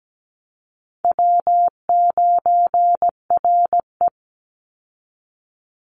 Morse code, heard as a keyed tone, about 17 wpm, 700 Hz. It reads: W9RE
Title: Easy Call Signs Repeated 2x - 17wpm